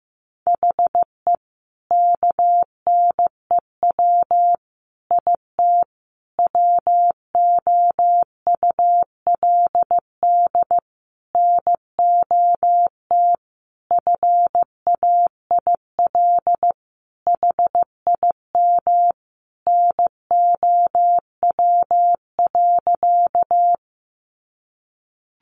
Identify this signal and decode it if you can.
Morse code, heard as a keyed tone, about 15 wpm, 700 Hz. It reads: HE KNEW IT WOULD NOT FAIL HIM NOW.
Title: Sentences from Top 1100 Words 15wpm